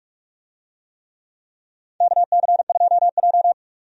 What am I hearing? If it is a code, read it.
Morse code, heard as a keyed tone, about 45 wpm, 700 Hz. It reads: KC2J
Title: USA Call Signs - 4 Characters - 45wpm